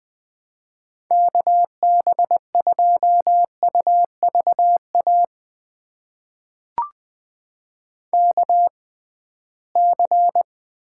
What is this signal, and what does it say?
Morse code, heard as a keyed tone, about 20 wpm, spaced as for 15 wpm, 700 Hz. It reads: KB2UVA KC
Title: USA Call Signs - 6 Characters - 6x Character Spacing - 20wpm